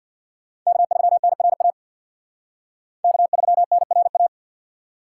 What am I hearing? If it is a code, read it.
Morse code, heard as a keyed tone, about 63 wpm, 700 Hz. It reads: K3NRA K3NRA